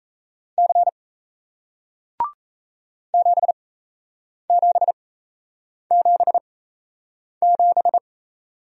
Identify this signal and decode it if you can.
Morse code, heard as a keyed tone, about 37 wpm, spaced as for 8 wpm, 700 Hz. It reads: C 7777